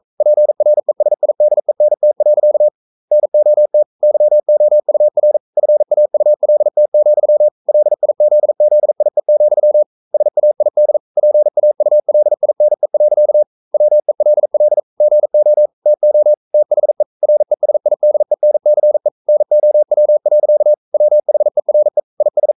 Morse code, heard as a keyed tone, about 42 wpm, 600 Hz. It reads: PRESIDENT. NOT YOUR FAULT, LIZZIE, SAID PAULINE. WELL GO TO THE RESIDENCE NOW. WHERE IS
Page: Romanen Never af Ken Follet på engelsk.